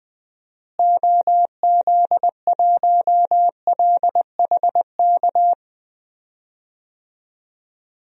Morse code, heard as a keyed tone, about 20 wpm, 700 Hz. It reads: OZ1LHK
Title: DXCC-100 Call Signs - 6 Characters - Repeated 2x - 20wpm